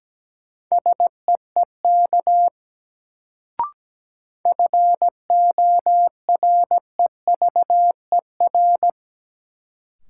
Morse code, heard as a keyed tone, about 17 wpm, 700 Hz. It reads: SEEK FOREVER
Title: Sentences from Top 1600 Words - New Word Drill Alt - 17wpm